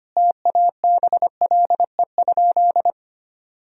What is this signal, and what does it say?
Morse code, heard as a keyed tone, about 25 wpm, 700 Hz. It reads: TABLE?
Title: Sentences from Top 1000 Words - Review Part 3 - 25wpm